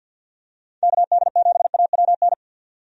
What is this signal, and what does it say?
Morse code, heard as a keyed tone, about 48 wpm, 700 Hz. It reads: KD7AWN
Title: USA Call Signs - 6 Characters - 50wpm